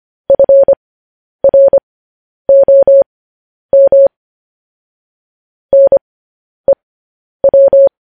Morse code, heard as a keyed tone, about 25 wpm, spaced as for 10 wpm, 550 Hz. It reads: FROM NEW
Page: news headlines in morse code at 10 wpm